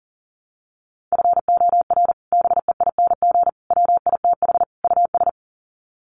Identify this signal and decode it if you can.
Morse code, heard as a keyed tone, about 40 wpm, 700 Hz. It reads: FOR BEING WITH US
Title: Sets of 4 Words 40wpm